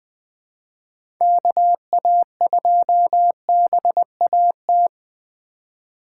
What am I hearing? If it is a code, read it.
Morse code, heard as a keyed tone, about 20 wpm, 700 Hz. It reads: KA2BAT
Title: USA Call Signs - 6 Characters - 3x Character Spacing - 20wpm